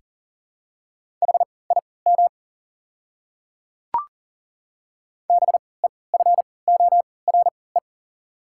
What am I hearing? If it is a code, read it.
Morse code, heard as a keyed tone, about 40 wpm, spaced as for 23 wpm, 690 Hz. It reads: HIM BEFORE
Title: 3x Character Spacing - Top 200 Words - 40wpm